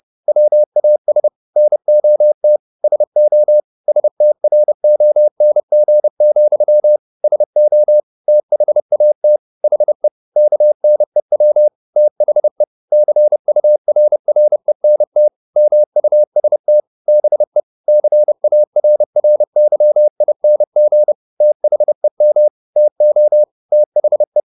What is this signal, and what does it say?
Morse code, heard as a keyed tone, about 30 wpm, 600 Hz. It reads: WAS NOT SO STRONG, SO THAT HE KNEW THE CURRENT MUST BE CARRYING THEM TO THE
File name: the old man and the sea / the old man and the sea 1 30wpm